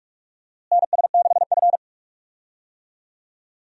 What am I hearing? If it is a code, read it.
Morse code, heard as a keyed tone, about 46 wpm, 700 Hz. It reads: NS6F
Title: USA Call Signs - 4 Characters - Repeated 3x - 45wpm